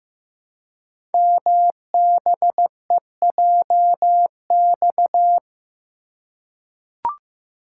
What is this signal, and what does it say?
Morse code, heard as a keyed tone, about 15 wpm, 700 Hz. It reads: MBEJX
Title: Code Groups - 5 Letters - 15wpm